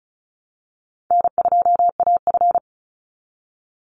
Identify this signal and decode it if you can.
Morse code, heard as a keyed tone, about 34 wpm, 700 Hz. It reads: N2AF
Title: NA SOTA Chaser Callsigns 35wpm